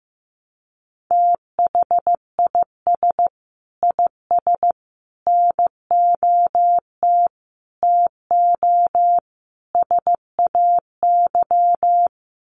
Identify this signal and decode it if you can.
Morse code, heard as a keyed tone, about 15 wpm, 700 Hz. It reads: THIS IS NOT TO SAY
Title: Sets of 5 Words Encore 15wpm